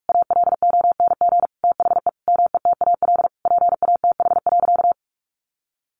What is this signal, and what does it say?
Morse code, heard as a keyed tone, about 44 wpm, 700 Hz. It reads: ALONG THE METAL PATH.
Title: A Sound Of Thunder 45wpm